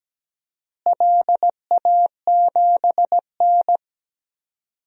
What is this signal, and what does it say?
Morse code, heard as a keyed tone, about 17 wpm, 700 Hz. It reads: LA7N
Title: DXCC-100 Call Signs - 4 Characters - Repeated 3x - 17wpm